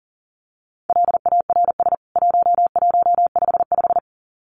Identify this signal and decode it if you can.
Morse code, heard as a keyed tone, about 40 wpm, 700 Hz. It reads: LARS 1155
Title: CWT Exchange 40wpm